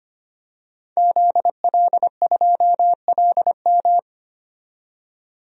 Morse code, heard as a keyed tone, about 25 wpm, 700 Hz. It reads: ZL2LM
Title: DXCC-100 Call Signs - 5 Characters - 25wpm